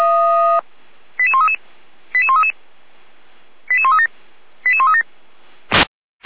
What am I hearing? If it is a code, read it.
Five-tone selective call (ZVEI), heard as quick runs of five tones, tones of about 70 ms. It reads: 88133 88133 88137 88137